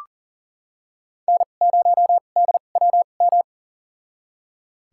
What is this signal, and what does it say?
Morse code, heard as a keyed tone, about 40 wpm, spaced as for 29 wpm, 700 Hz. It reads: N0DWM
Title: USA Call Signs - 5 Characters - 2x Character Spacing - 40wpm